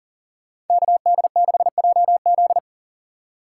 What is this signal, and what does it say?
Morse code, heard as a keyed tone, about 53 wpm, 700 Hz. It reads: KD6JZ